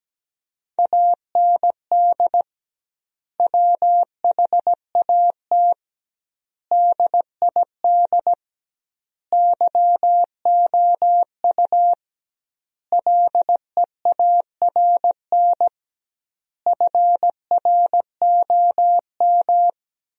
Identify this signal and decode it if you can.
Morse code, heard as a keyed tone, about 17 wpm, 700 Hz. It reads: AND WHAT DID YOU LEARN FROM
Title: 2x Word-Spacing - Sentences from Top 300 Words - 17wpm